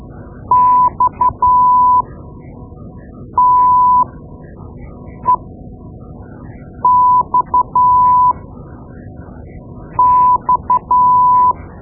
Morse code, heard as a keyed tone, about 12 wpm, spaced as for 5 wpm, 1000 Hz. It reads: XTEXX
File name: Morse Lessons / Morse Lessons / Tape-2B